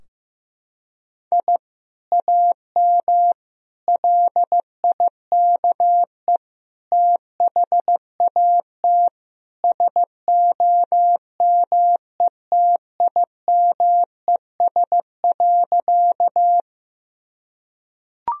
Morse code, heard as a keyed tone, about 16 wpm, 700 Hz. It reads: I AM LIKE THAT SOMETIMES.
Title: Sentences from Top 1000 Words - Review Part 3 - 15wpm